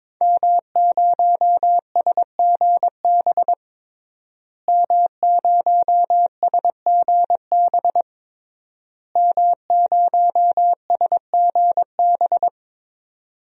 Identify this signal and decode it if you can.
Morse code, heard as a keyed tone, about 22 wpm, 700 Hz. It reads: M0SGB M0SGB M0SGB
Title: DXCC-100 Call Signs - 5 Characters - Repeated 3x - 22wpm